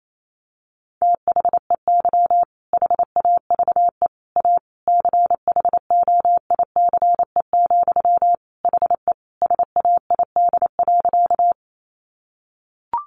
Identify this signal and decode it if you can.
Morse code, heard as a keyed tone, about 28 wpm, 690 Hz. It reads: THEY HAVE A CHOICE, HE SAID.
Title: Sentences from Top 900 Words 28wpm